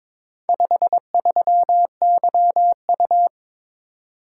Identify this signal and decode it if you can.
Morse code, heard as a keyed tone, about 22 wpm, 700 Hz. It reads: 53YU
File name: practice / 22wpm / Four Letter-Number Combinations - 22wpm